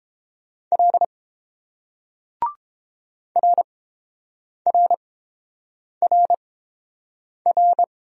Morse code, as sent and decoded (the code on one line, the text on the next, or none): .-.. / .-. .-. .-. .-.
L RRRR